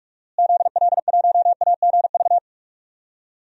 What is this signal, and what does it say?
Morse code, heard as a keyed tone, about 46 wpm, 700 Hz. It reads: ZL1AGV
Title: DXCC-100 Call Signs - 6 Characters - Repeated 3x - 45wpm